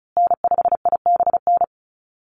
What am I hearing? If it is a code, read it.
Morse code, heard as a keyed tone, about 34 wpm, 700 Hz. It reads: N5IBN
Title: USA Call Signs Repeated 2x 35wpm